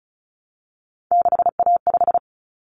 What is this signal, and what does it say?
Morse code, heard as a keyed tone, about 34 wpm, 700 Hz. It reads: 6A5
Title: Common Cut Numbers - 3-Digit 35wpm